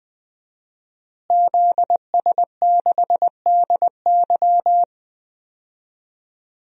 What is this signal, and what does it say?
Morse code, heard as a keyed tone, about 20 wpm, 700 Hz. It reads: ZS6DY
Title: DXCC-100 Call Signs - 5 Characters - Repeated 2x - 20wpm